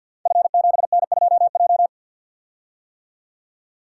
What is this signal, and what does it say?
Morse code, heard as a keyed tone, about 52 wpm, 700 Hz. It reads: F7N2J
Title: Code Groups - 5 Letter:Numbers - 50wpm